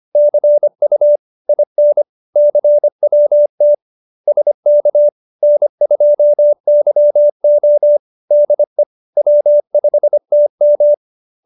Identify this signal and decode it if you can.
Morse code, heard as a keyed tone, about 25 wpm, 590 Hz. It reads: CU IN CWT SK N2YO DE W5TM